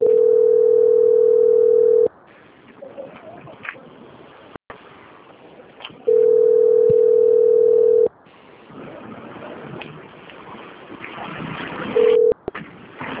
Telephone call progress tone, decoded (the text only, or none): ringback tone